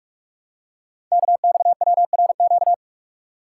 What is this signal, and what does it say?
Morse code, heard as a keyed tone, about 44 wpm, 700 Hz. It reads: KXWRQ
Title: Code Groups - 5 Letters - 45wpm